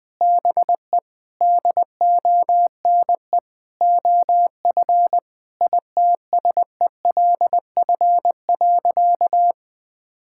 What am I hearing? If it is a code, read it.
Morse code, heard as a keyed tone, about 20 wpm, 700 Hz. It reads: BE DONE OF ITSELF.
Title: Sentences from Top 1000 Words - Review Part 5 - 20wpm